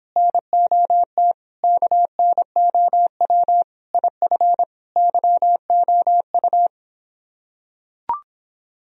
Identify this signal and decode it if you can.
Morse code, heard as a keyed tone, about 26 wpm, 700 Hz. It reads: NOT KNOW IF YOU